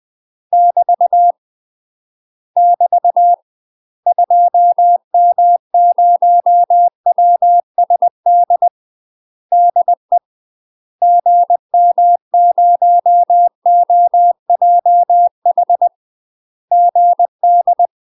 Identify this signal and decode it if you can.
Morse code, heard as a keyed tone, about 20 wpm, 700 Hz. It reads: <BT> <BT> 2M0WSD DE GM0OJH GD